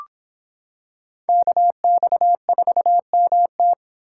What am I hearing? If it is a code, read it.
Morse code, heard as a keyed tone, about 26 wpm, 700 Hz. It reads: KX4MT